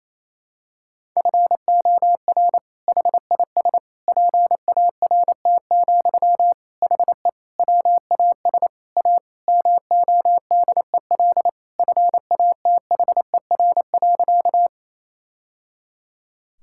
Morse code, heard as a keyed tone, about 28 wpm, 700 Hz. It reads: FOR HIS PART, HE WAS A MODEL FATHER.
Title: Sentences from Top 1000 Words - Review Part 5 - 28wpm